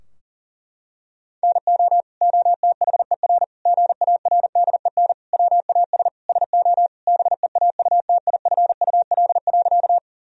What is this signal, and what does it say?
Morse code, heard as a keyed tone, about 37 wpm, 700 Hz. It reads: NO OTHER GARDEN WAS SO BEAUTIFUL.
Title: Sentences from Top 1000 Words - Review Part 4 - 40wpm